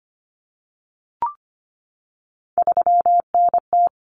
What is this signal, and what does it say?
Morse code, heard as a keyed tone, about 25 wpm, 700 Hz. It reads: 3NT
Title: Common Cut Numbers - 3-Digit 25wpm